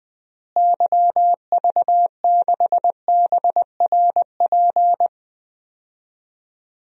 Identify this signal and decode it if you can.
Morse code, heard as a keyed tone, about 20 wpm, 700 Hz. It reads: YV6BRP
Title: DXCC-100 Call Signs - 6 Characters - Repeated 2x - 20wpm